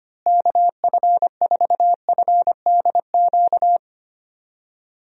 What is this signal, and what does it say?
Morse code, heard as a keyed tone, about 25 wpm, 700 Hz. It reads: KF4FDQ
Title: USA Call Signs - 6 Characters - Repeated 3x - 25wpm